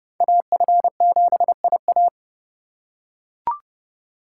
Morse code, heard as a keyed tone, about 28 wpm, 700 Hz. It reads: AF7IA